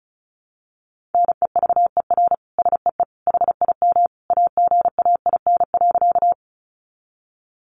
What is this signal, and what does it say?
Morse code, heard as a keyed tone, about 34 wpm, 700 Hz. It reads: NEVER SEE HIM AGAIN.
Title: Sentences from Top 300 Words 35wpm